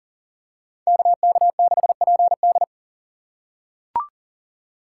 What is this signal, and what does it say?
Morse code, heard as a keyed tone, about 40 wpm, 700 Hz. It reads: KK6PD
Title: USA Call Signs - 5 Characters - 6x Character Spacing - 40wpm